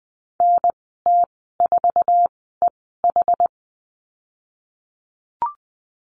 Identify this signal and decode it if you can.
Morse code, heard as a keyed tone, about 20 wpm, spaced as for 15 wpm, 700 Hz. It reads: NT4EH
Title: 062 Lesson 11g - Letter h - Call Signs 20wpm